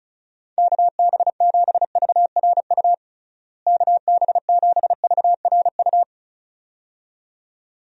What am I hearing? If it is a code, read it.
Morse code, heard as a keyed tone, about 36 wpm, 700 Hz. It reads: KB7VRU KB7VRU
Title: USA Call Signs - 6 Characters - Repeated 2x - 35wpm